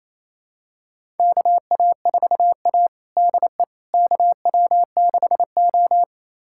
Morse code, heard as a keyed tone, about 28 wpm, 700 Hz. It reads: KA4A DE KW6O
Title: USA Callsign DE Callsign - 4 Characters - 28wpm